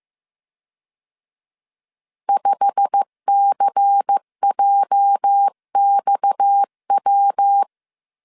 Morse code, heard as a keyed tone, about 15 wpm, 800 Hz. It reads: 5CJXW
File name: CW / DET / 1 9 1500C S C 14wpm 3245